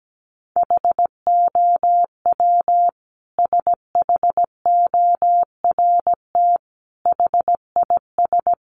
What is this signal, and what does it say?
Morse code, heard as a keyed tone, about 17 wpm, 700 Hz. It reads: HOW SHORT HIS
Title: Sentences from Top 400 Words 17wpm